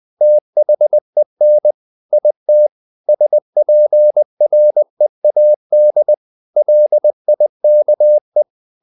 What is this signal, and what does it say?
Morse code, heard as a keyed tone, about 20 wpm, 600 Hz. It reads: THEN IT SPREAD LIKE
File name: the old man and the sea part 2 20wpm